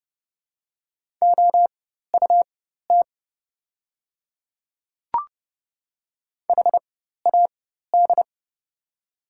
Morse code, heard as a keyed tone, about 29 wpm, spaced as for 14 wpm, 690 Hz. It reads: OUT HAD
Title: 4x Character Spacing - Top 200 Words - 30wpm